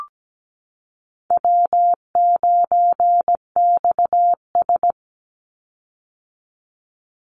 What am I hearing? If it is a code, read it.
Morse code, heard as a keyed tone, about 17 wpm, 700 Hz. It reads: W9XS
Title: Call Signs - 17wpm